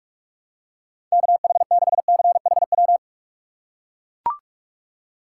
Morse code, heard as a keyed tone, about 42 wpm, 700 Hz. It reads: KH6CHW
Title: USA Call Signs - 6 Characters - 4x Character Spacing - 45wpm